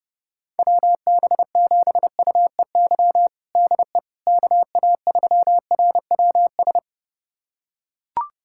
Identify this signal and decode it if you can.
Morse code, heard as a keyed tone, about 31 wpm, 700 Hz. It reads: WB7UEY DE KA3RWS
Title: USA Callsign DE Callsign - 6 Characters - 30wpm